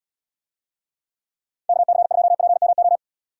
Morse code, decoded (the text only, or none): KC2PMO